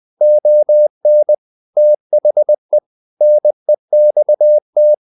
Morse code, heard as a keyed tone, about 20 wpm, 600 Hz. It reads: ON THE NEXT
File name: the old man and the sea part 2 20wpm